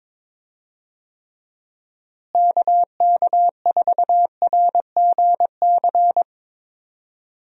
Morse code, heard as a keyed tone, about 22 wpm, 700 Hz. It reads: KK4RGC
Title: USA Call Signs - 6 Characters - Repeated 3x - 22wpm